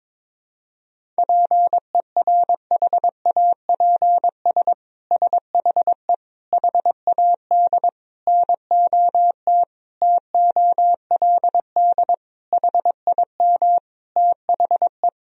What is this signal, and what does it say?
Morse code, heard as a keyed tone, about 22 wpm, 700 Hz. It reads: PERHAPS SHE HAD NOT TOLD HIM THE
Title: Sentences from Top 1000 Words - Review Part 1 - 22wpm